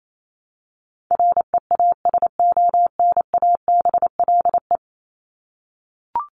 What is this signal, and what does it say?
Morse code, heard as a keyed tone, about 28 wpm, 700 Hz. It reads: REASONABLE
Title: Top 1501-1750 Words 28wpm